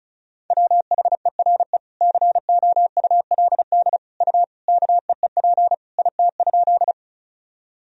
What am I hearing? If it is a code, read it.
Morse code, heard as a keyed tone, about 34 wpm, 700 Hz. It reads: WHERE COULD U KEEP IT?
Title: Sentences from Top 300 Words and Abbreviations 35wpm